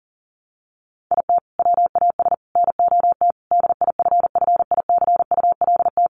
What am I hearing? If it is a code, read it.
Morse code, heard as a keyed tone, about 40 wpm, 700 Hz. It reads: IT WAS NOT DIFFICULT
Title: Sentences from Top 1200 Words 40wpm